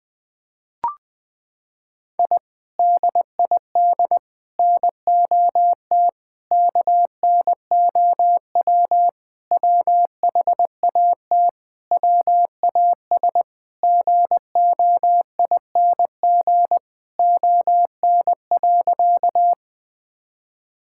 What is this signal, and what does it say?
Morse code, heard as a keyed tone, about 20 wpm, 700 Hz. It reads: I DID NOT KNOW WHAT WAS GOING ON.
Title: Sentences with 4-Word Phrases - 20wpm